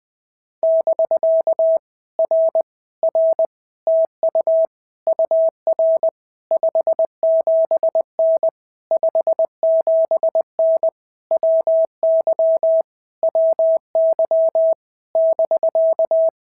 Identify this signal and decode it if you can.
Morse code, heard as a keyed tone, about 19 wpm, 650 Hz. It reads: <BK> R R TU UR 57N 57N WY WY <BK>
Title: POTA QSO Practice - 20wpm